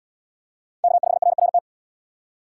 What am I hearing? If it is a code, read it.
Morse code, heard as a keyed tone, about 77 wpm, 700 Hz. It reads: K5RRT